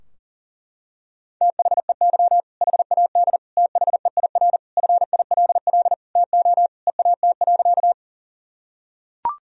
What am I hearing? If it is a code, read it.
Morse code, heard as a keyed tone, about 37 wpm, 700 Hz. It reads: THEY HAD THEIR FILL TO EAT.
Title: Sentences from Top 1000 Words - Review Part 3 - 40wpm